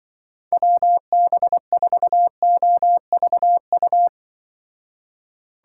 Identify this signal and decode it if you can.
Morse code, heard as a keyed tone, about 24 wpm, 700 Hz. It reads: WB4OVU
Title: Speed Racing - USA 2x3 Call Signs 20wpm